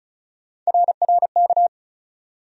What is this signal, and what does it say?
Morse code, heard as a keyed tone, about 34 wpm, 700 Hz. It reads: RRK
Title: USA Call Sign Postfixes - 3 Letters 35wpm